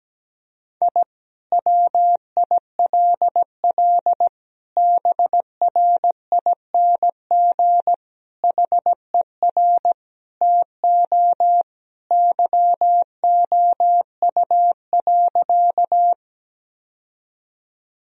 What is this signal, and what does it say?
Morse code, heard as a keyed tone, about 17 wpm, 700 Hz. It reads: I WILL BRING HER TO YOU.
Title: Sentences from Top 1000 Words - Review Part 3 - 17wpm